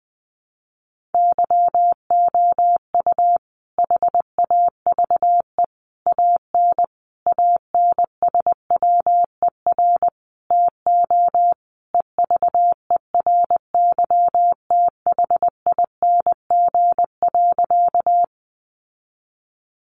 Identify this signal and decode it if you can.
Morse code, heard as a keyed tone, about 20 wpm, 700 Hz. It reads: YOU HAVE AN ANSWER TO EVERYTHING.
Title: Sentences from Top 700 Words 20wpm